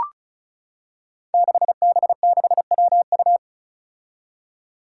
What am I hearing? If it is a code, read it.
Morse code, heard as a keyed tone, about 34 wpm, 700 Hz. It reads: 6B6WU
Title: Five Challenging Letter-Number Combinations 35wpm